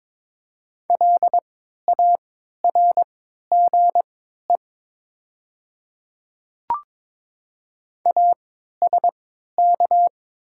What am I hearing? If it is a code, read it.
Morse code, heard as a keyed tone, about 22 wpm, spaced as for 12 wpm, 700 Hz. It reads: LARGE ASK
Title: 3x Character Spacing - Top 200 Words - 22wpm